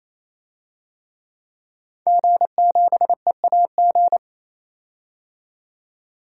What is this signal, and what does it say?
Morse code, heard as a keyed tone, about 28 wpm, 700 Hz. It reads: G7EAG
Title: DXCC-100 Call Signs - 5 Characters - 28wpm